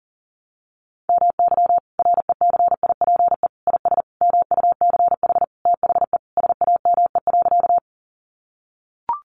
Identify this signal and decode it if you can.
Morse code, heard as a keyed tone, about 40 wpm, 700 Hz. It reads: MY RECIPE IS MUCH THE SAME.
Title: Sentences from Top 1200 Words 40wpm